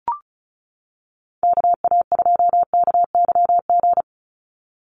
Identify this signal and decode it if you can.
Morse code, heard as a keyed tone, about 34 wpm, 700 Hz. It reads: KA2KYG
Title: USA Call Signs 35wpm